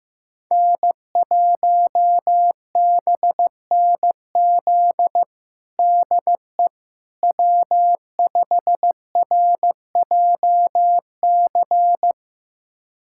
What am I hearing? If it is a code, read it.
Morse code, heard as a keyed tone, about 15 wpm, 700 Hz. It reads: N1BNZ DE W5RJC
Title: USA Callsign DE Callsign - 5 Characters - 15wpm